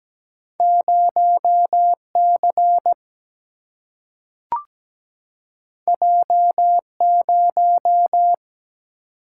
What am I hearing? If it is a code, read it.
Morse code, heard as a keyed tone, about 17 wpm, 700 Hz. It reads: 0 C J 0
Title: Single-Letter Single-Number 17wpm